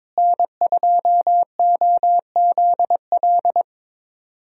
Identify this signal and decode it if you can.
Morse code, heard as a keyed tone, about 22 wpm, 700 Hz. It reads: N2OZL
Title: USA Call Signs - 5 Characters - Repeated 3x - 22wpm